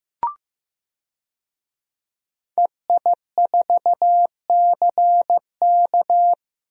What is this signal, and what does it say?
Morse code, heard as a keyed tone, about 15 wpm, 700 Hz. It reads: EI4CK
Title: DXCC-100 Call Signs - 5 Characters - Repeated 3x - 15wpm